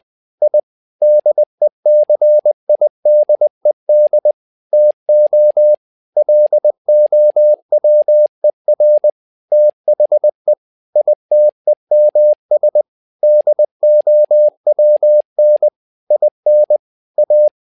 Morse code, heard as a keyed tone, about 20 wpm, 600 Hz. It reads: I DECIDED TO LOWER THE ITEMS DOWN IN A
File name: tower trouble 20wpm